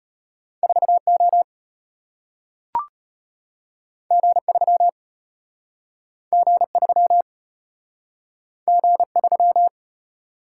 Morse code, heard as a keyed tone, about 34 wpm, 700 Hz. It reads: 4O G3 G3 G3